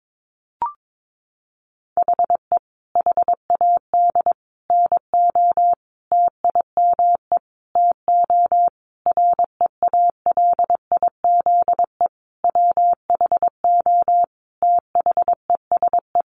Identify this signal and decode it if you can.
Morse code, heard as a keyed tone, about 22 wpm, 700 Hz. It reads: HE HAD NO TIME TO REALIZE WHO THESE
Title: Sentences from Top 1000 Words 22wpm